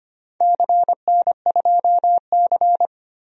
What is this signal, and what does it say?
Morse code, heard as a keyed tone, about 25 wpm, 700 Hz. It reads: CN2C
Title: DXCC-100 Call Signs - 4 Characters - 25wpm